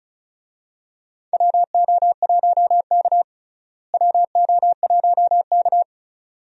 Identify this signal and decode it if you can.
Morse code, heard as a keyed tone, about 36 wpm, 700 Hz. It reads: WO1K WO1K
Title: USA Call Signs - 4 Characters - Repeated 2x - 35wpm